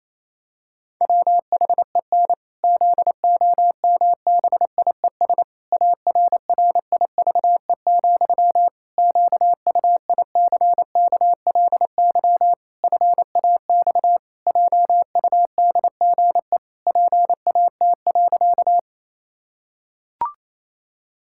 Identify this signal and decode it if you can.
Morse code, heard as a keyed tone, about 28 wpm, 700 Hz. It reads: WHEN ZOMBIES ARRIVE, QUICKLY FAX JUDGE PAT.
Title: Pangram Challenge v2 - 28wpm